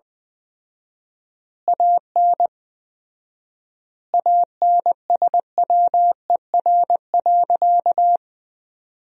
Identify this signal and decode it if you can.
Morse code, heard as a keyed tone, about 20 wpm, 700 Hz. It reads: AN ANSWER.
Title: 4x Word-Spacing - Sentences from Top 300 Words - 20wpm